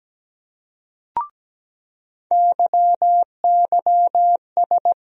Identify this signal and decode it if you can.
Morse code, heard as a keyed tone, about 17 wpm, 700 Hz. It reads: YYS
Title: USA Call Sign Postfixes - 3 Letters 17wpm